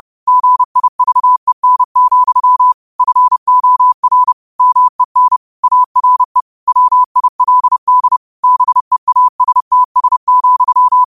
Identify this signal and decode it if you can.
Morse code, heard as a keyed tone, about 30 wpm, 1000 Hz. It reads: GIVEN, FOR MEN ARE WILD BEASTS,
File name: qotdfiles / 30wpm